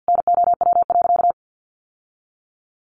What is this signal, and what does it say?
Morse code, heard as a keyed tone, about 50 wpm, 700 Hz. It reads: NOW.
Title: Sentences from Top 600 Words 50wpm